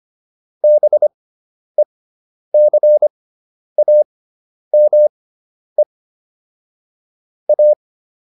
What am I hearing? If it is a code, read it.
Morse code, heard as a keyed tone, about 25 wpm, spaced as for 10 wpm, 600 Hz. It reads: BECAME A